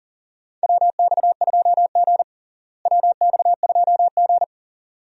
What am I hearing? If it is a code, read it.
Morse code, heard as a keyed tone, about 40 wpm, 700 Hz. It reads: WX2G WX2G
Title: USA Call Signs - 4 Characters - Repeated 2x - 40wpm